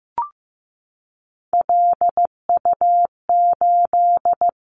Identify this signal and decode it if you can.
Morse code, heard as a keyed tone, about 15 wpm, 700 Hz. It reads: LU8
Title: Callsign Prefixes for Top 100 DXCC - 15wpm